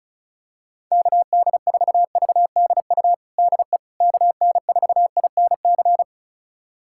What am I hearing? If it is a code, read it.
Morse code, heard as a keyed tone, about 36 wpm, 700 Hz. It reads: KD4VDU DE KN4INC
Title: USA Callsign DE Callsign - 6 Characters - 35wpm